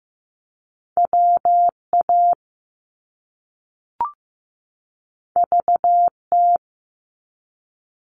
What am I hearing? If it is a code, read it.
Morse code, heard as a keyed tone, about 15 wpm, 700 Hz. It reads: WA VT
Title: US States 15wpm